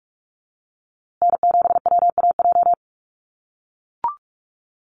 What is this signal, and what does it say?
Morse code, heard as a keyed tone, about 44 wpm, 700 Hz. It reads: N7WAJ
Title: USA Call Signs Repeated 2x 45wpm